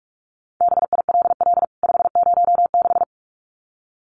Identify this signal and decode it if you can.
Morse code, heard as a keyed tone, about 44 wpm, 700 Hz. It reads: BILL 506
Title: CWT Exchange 45wpm